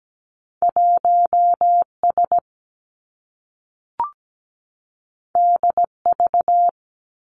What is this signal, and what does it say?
Morse code, heard as a keyed tone, about 17 wpm, 700 Hz. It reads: 1 S D V
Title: Two Challenging Letter-Number Combinations 17wpm